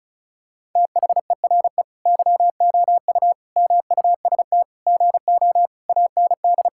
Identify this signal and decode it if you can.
Morse code, heard as a keyed tone, about 36 wpm, 700 Hz. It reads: THERE YOU MUST GO AND
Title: Sentences from Top 1600 Words 35wpm